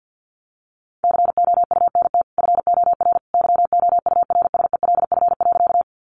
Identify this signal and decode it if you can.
Morse code, heard as a keyed tone, about 50 wpm, 700 Hz. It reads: COUNT FOR YOURSELF.
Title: Sentences from Top 1500 Words 50wpm